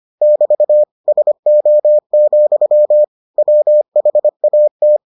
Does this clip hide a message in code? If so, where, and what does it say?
Yes: Morse code from 0.2 to 5.0 s, heard as a keyed tone, about 25 wpm, 600 Hz. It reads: <BT> SO, WHAT